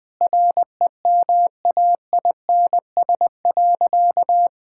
Morse code, heard as a keyed tone, about 20 wpm, 700 Hz. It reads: REMAINS.
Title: Sentences with 2-Word Phrases - Spoken Beginning - 20wpm